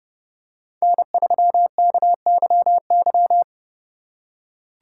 Morse code, heard as a keyed tone, about 30 wpm, 700 Hz. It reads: N3KYY
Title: USA Call Signs - 6x Character Spacing - 30wpm